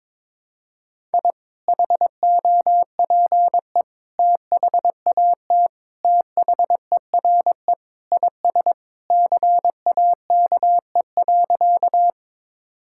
Morse code, heard as a keyed tone, about 22 wpm, 700 Hz. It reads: I HOPE THAT THERE IS CAKE.
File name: practice / 22wpm / Sentences with 2-Word Phrases - 22wpm